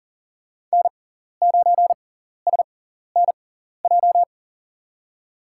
Morse code, heard as a keyed tone, about 37 wpm, spaced as for 14 wpm, 690 Hz. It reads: N9SNJ
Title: USA Call Signs - 5 Characters - 6x Character Spacing - 40wpm